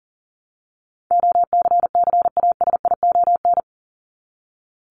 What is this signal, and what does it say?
Morse code, heard as a keyed tone, about 40 wpm, 700 Hz. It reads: OCCASION
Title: Top 1501-1750 Words 40wpm